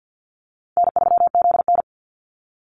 Morse code, heard as a keyed tone, about 49 wpm, 700 Hz. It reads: N3ZN
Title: Call Signs - 50wpm